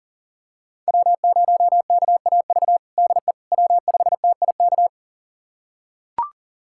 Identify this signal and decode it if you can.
Morse code, heard as a keyed tone, about 40 wpm, 700 Hz. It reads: W0KAV DE W5TIK
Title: USA Callsign DE Callsign - 5 Characters - 40wpm